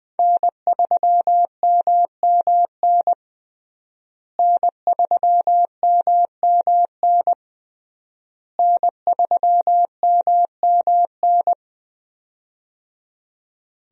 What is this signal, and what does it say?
Morse code, heard as a keyed tone, about 20 wpm, 700 Hz. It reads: N3MMN N3MMN N3MMN
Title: USA Call Signs - 5 Characters - Repeated 3x - 20wpm